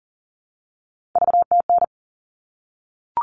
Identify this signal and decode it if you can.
Morse code, heard as a keyed tone, about 40 wpm, 700 Hz. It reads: VTN
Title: All Cut Numbers - 3-Digit 40wpm